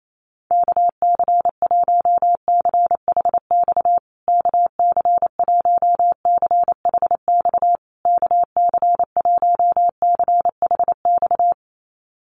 Morse code, heard as a keyed tone, about 28 wpm, 700 Hz. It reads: KC1CHX KC1CHX KC1CHX
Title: USA Call Signs Repeated 3x 28wpm